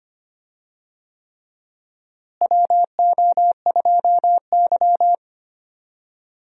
Morse code, heard as a keyed tone, about 25 wpm, 700 Hz. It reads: WO2Y
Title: USA Call Signs - 4 Characters - Repeated 3x - 25wpm